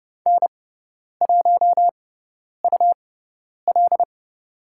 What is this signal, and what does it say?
Morse code, heard as a keyed tone, about 30 wpm, spaced as for 10 wpm, 700 Hz. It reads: N1UL
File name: Lesson 17g - Letter u - Call Signs 30wpm